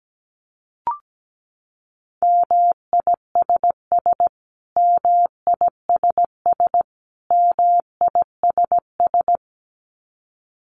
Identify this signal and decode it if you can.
Morse code, heard as a keyed tone, about 17 wpm, 700 Hz. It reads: MISS MISS MISS
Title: Top 500 Words Repeated 17wpm